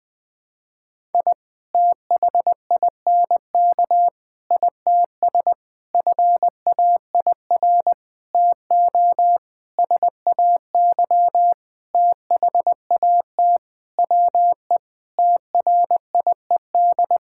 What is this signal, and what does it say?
Morse code, heard as a keyed tone, about 20 wpm, 700 Hz. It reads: I THINK ITS FAIR TO SAY THAT WE TRIED
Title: Sentences with 7-Word Phrases - 20wpm